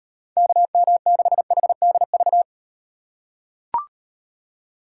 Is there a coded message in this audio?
Yes: Morse code from 0.4 to 2.4 s, heard as a keyed tone, about 35 wpm, 700 Hz. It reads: KM6HDV